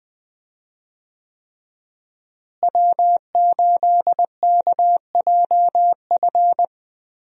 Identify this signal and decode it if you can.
Morse code, heard as a keyed tone, about 20 wpm, 700 Hz. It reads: W8KJF
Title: USA Call Signs - 5 Characters - Repeated 3x - 20wpm